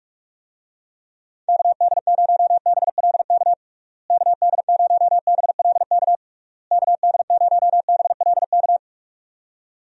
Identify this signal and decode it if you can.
Morse code, heard as a keyed tone, about 46 wpm, 700 Hz. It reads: KD0BLK KD0BLK KD0BLK
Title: USA Call Signs - 6 Characters - Repeated 3x - 45wpm